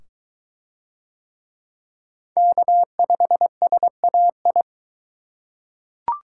Morse code, heard as a keyed tone, about 22 wpm, 700 Hz. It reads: K5SAI